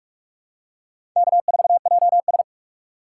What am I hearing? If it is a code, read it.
Morse code, heard as a keyed tone, about 43 wpm, 700 Hz. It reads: K4JS
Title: USA Call Signs - 4 Characters - 2x Character Spacing - 45wpm